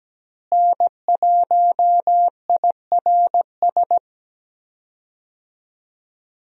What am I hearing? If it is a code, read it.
Morse code, heard as a keyed tone, about 17 wpm, 700 Hz. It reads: N1IRS
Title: USA Call Signs - 5 Characters - Repeated 3x - 17wpm